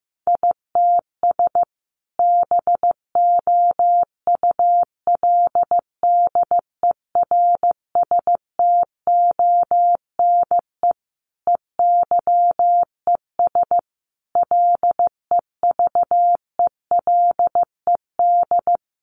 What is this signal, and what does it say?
Morse code, heard as a keyed tone, about 15 wpm, 700 Hz. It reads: ITS BOULDERSTONE EYES LEVELED
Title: A Sound Of Thunder 15wpm